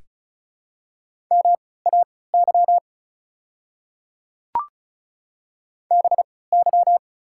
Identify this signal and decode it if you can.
Morse code, heard as a keyed tone, about 36 wpm, spaced as for 20 wpm, 700 Hz. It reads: MAY BY
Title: 3x Character Spacing - Top 100 Words - 35wpm